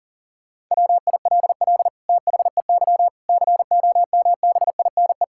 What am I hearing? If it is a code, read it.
Morse code, heard as a keyed tone, about 40 wpm, 700 Hz. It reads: WILL THEY COMBINE
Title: Sentences from Top 1600 Words 40wpm